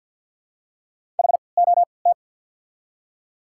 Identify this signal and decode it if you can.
Morse code, heard as a keyed tone, about 53 wpm, spaced as for 29 wpm, 690 Hz. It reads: HOT
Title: 3x Character Spacing - Top 100 Words - 50wpm